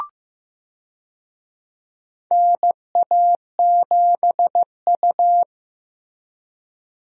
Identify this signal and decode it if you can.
Morse code, heard as a keyed tone, about 15 wpm, 700 Hz. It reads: NA7U
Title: USA Call Signs - 4 Characters - Repeated 2x - 15wpm